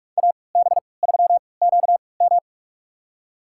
Morse code, heard as a keyed tone, about 46 wpm, spaced as for 25 wpm, 700 Hz. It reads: AB3QM
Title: USA Call Signs - 5 Characters - 3x Character Spacing - 45wpm